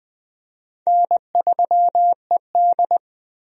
Morse code, heard as a keyed tone, about 20 wpm, 700 Hz. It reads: N3ED
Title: USA Call Signs - 4 Characters - Repeated 3x - 20wpm